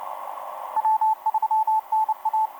Morse code, heard as a keyed tone, about 30 wpm, 870 Hz, with radio noise. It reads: W3NA
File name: hr5-f2jd la6kea etc 17m cw 28mar2011 2117